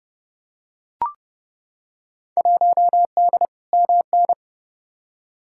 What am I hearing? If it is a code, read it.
Morse code, heard as a keyed tone, about 30 wpm, 700 Hz. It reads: 1D MN
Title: Field Day Exchanges - 30wpm